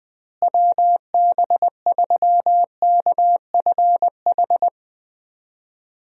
Morse code, heard as a keyed tone, about 20 wpm, 700 Hz. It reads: WB3KFH
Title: USA Call Signs - 6 Characters - Repeated 2x - 20wpm